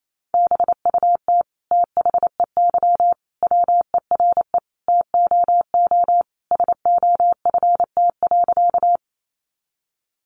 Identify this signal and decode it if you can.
Morse code, heard as a keyed tone, about 28 wpm, 700 Hz. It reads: BUT THEY WERE TOO SOFT.
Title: Sentences from Top 1500 Words 28wpm